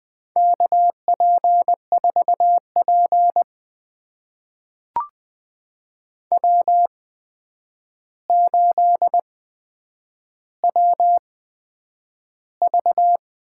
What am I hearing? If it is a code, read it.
Morse code, heard as a keyed tone, about 20 wpm, spaced as for 9 wpm, 700 Hz. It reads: KP4P W8WV
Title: USA Call Signs - 4 Characters - 8x Character Spacing - 20wpm